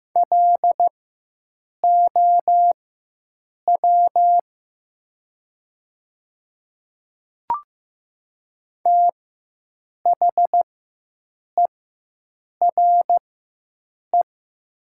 Morse code, heard as a keyed tone, about 15 wpm, spaced as for 7 wpm, 700 Hz. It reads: LOW THERE
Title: 4x Character Spacing - Top 300 Words - 15wpm